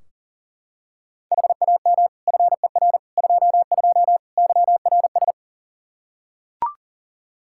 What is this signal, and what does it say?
Morse code, heard as a keyed tone, about 40 wpm, 700 Hz. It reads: HAM FER 22 YRS
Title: Practice QSO - Exchange 3 - More Info - 40wpm